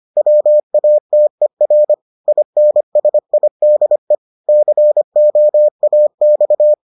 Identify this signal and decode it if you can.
Morse code, heard as a keyed tone, about 25 wpm, 600 Hz. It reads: WATER INSIDE COAX